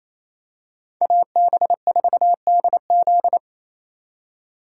Morse code, heard as a keyed tone, about 27 wpm, 700 Hz. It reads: AB4DZ
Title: USA Call Signs - 3x Character Spacing - 28wpm